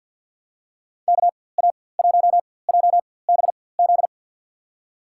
Morse code, heard as a keyed tone, about 51 wpm, spaced as for 23 wpm, 700 Hz. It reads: KA1JBZ
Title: USA Call Signs - 4x Character Spacing - 50wpm